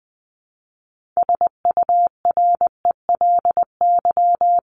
Morse code, heard as a keyed tone, about 20 wpm, 700 Hz. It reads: SURELY
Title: LY Suffix - Top 100 Words with Base Words 20wpm